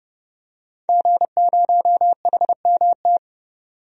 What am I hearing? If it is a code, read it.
Morse code, heard as a keyed tone, about 30 wpm, 700 Hz. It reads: G0HMT
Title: DXCC-100 Call Signs - 5 Characters - 30wpm